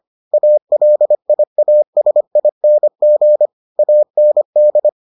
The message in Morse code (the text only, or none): ALIASING AND